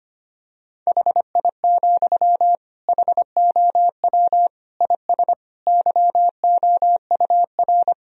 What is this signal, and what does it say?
Morse code, heard as a keyed tone, about 25 wpm, 700 Hz. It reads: HI, HOW IS YOUR
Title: Sentences from Top 1000 Words - Review Part 3 - 25wpm